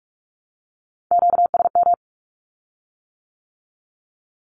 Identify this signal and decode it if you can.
Morse code, heard as a keyed tone, about 46 wpm, 700 Hz. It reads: QSM
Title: Q-Codes with Spoken Meaning - 45wpm